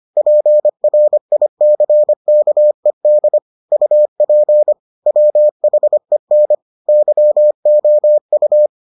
Morse code, heard as a keyed tone, about 25 wpm, 600 Hz. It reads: PRICKED UP WHEN YOU